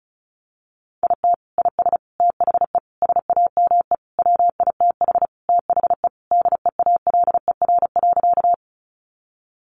Morse code, heard as a keyed tone, about 34 wpm, 700 Hz. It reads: IT IS THE SAME WITH THE DEALER.
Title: Sentences from Top 1100 Words 35wpm